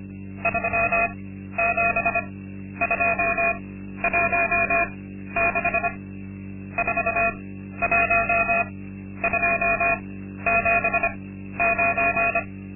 Morse code, heard as a keyed tone, about 25 wpm, spaced as for 12 wpm, 680 Hz. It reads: 37216 41279
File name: Linksmakalnis-2